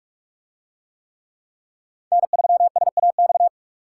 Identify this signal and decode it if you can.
Morse code, heard as a keyed tone, about 43 wpm, 700 Hz. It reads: N3SAX